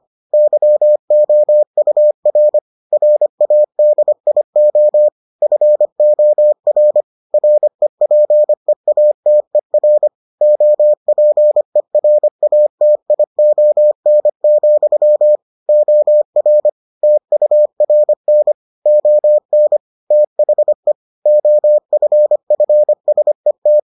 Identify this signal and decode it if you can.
Morse code, heard as a keyed tone, about 25 wpm, 600 Hz. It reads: YOUR RADIO FOR REPEATER OPERATION, OR TURN ON THE OFFSET